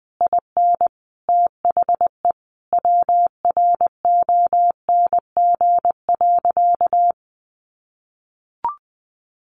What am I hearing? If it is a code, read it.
Morse code, heard as a keyed tone, about 20 wpm, 700 Hz. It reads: IN THE WRONG.
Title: Sentences from Top 1200 Words 20wpm